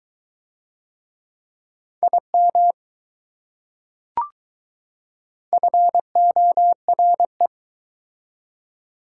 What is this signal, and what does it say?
Morse code, heard as a keyed tone, about 20 wpm, 700 Hz. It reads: IM FORE